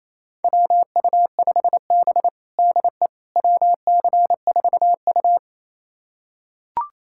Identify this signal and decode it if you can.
Morse code, heard as a keyed tone, about 28 wpm, 700 Hz. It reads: WU5B DE WC4U
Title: USA Callsign DE Callsign - 4 Characters - 28wpm